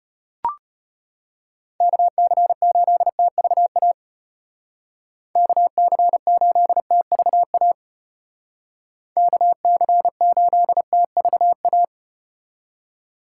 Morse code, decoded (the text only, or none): KC8TVA KC8TVA KC8TVA